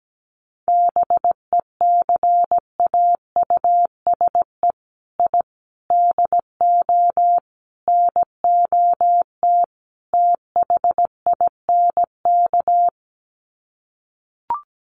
Morse code, heard as a keyed tone, about 17 wpm, 700 Hz. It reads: BECAUSE I DO NOT THINK
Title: Sets of 5 Words Encore 17wpm